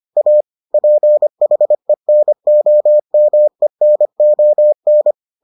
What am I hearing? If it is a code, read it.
Morse code, heard as a keyed tone, about 25 wpm, 600 Hz. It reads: A PHENOMENON